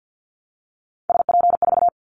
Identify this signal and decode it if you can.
Morse code, heard as a keyed tone, about 50 wpm, 700 Hz. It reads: SP4
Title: Callsign Prefixes for Top 25 DXCC - Repeated 3x - 50wpm